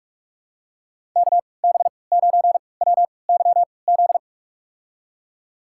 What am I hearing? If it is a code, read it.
Morse code, heard as a keyed tone, about 46 wpm, spaced as for 26 wpm, 690 Hz. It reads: KB9WYZ
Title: USA Call Signs - 6 Characters - 3x Character Spacing - 45wpm